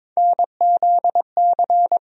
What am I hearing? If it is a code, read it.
Morse code, heard as a keyed tone, about 22 wpm, 700 Hz. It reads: NZC
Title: USA Call Sign Postfixes - 3 Letters 22wpm